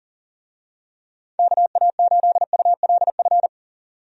Morse code, heard as a keyed tone, about 40 wpm, 700 Hz. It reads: KA8ULF
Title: USA Call Signs - 6 Characters - 40wpm